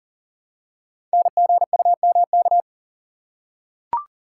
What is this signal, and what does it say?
Morse code, heard as a keyed tone, about 39 wpm, 700 Hz. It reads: NGUMK
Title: Code Groups - 5 Letters - 40wpm